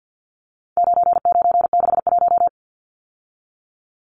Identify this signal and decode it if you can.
Morse code, heard as a keyed tone, about 50 wpm, 700 Hz. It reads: 9961
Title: Numbers - 4-Digit 50wpm